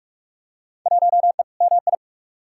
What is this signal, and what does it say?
Morse code, heard as a keyed tone, about 46 wpm, 700 Hz. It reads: 1E MI
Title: Field Day Exchanges - 45wpm